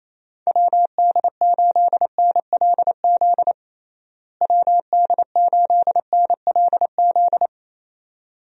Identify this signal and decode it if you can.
Morse code, heard as a keyed tone, about 28 wpm, 700 Hz. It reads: WD8NLZ WD8NLZ
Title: USA Call Signs - 6 Characters - Repeated 2x - 28wpm